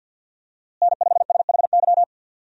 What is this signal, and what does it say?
Morse code, heard as a keyed tone, about 52 wpm, 700 Hz. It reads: N5SHY
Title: USA Call Signs - 5 Characters - Repeated 2x - 50wpm